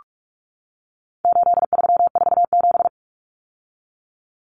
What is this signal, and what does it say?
Morse code, heard as a keyed tone, about 44 wpm, 700 Hz. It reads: 8347
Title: Numbers - 4-Digit 45wpm